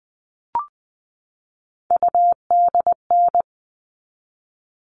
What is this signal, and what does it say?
Morse code, heard as a keyed tone, about 20 wpm, 700 Hz. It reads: UDN
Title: All Cut Numbers - 3-Digit 20wpm